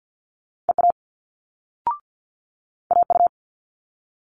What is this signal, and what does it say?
Morse code, heard as a keyed tone, about 50 wpm, 700 Hz. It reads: EA AU
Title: Multi-letter Phonemes Quick Fire 50wpm